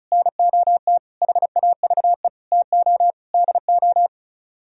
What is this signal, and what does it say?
Morse code, heard as a keyed tone, about 45 wpm, 700 Hz. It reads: NOT HAVE TO DO